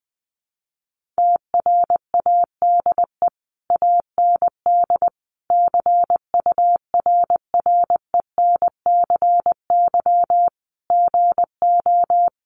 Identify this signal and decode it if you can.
Morse code, heard as a keyed tone, about 20 wpm, 700 Hz. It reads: TRADE AND CURRENCY GO
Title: Sentences from Top 1500 Words 20wpm